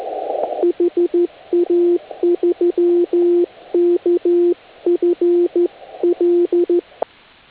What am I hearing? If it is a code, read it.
Morse code, heard as a keyed tone, about 13 wpm, 350 Hz. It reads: HA3KFL
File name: HA6A CW